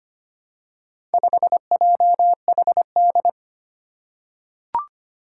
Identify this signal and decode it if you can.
Morse code, heard as a keyed tone, about 25 wpm, 700 Hz. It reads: 5JHD
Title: Four Challenging Letter-Number Combinations 25wpm